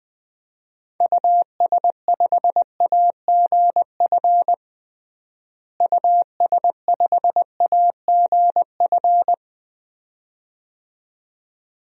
Morse code, heard as a keyed tone, about 20 wpm, 700 Hz. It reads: US5AGF US5AGF
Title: DXCC-100 Call Signs - 6 Characters - Repeated 2x - 20wpm